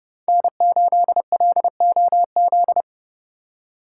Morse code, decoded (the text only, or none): N8LOZ